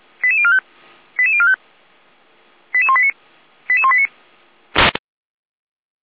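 Five-tone selective call (ZVEI), heard as quick runs of five tones, tones of about 70 ms. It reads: 88045 88045 88189 88189
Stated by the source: ZVEI five-tone call received by SDR, fire brigade Orlamünde (feuerwehr-orlamuende).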